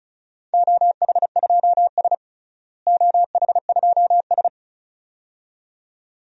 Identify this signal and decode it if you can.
Morse code, heard as a keyed tone, about 36 wpm, 700 Hz. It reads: OH2S OH2S
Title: DXCC-100 Call Signs - 4 Characters - Repeated 2x - 35wpm